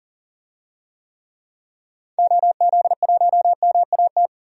Morse code, heard as a keyed tone, about 38 wpm, 700 Hz. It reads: OZ1MAT
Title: DXCC-100 Call Signs - 6 Characters - 40wpm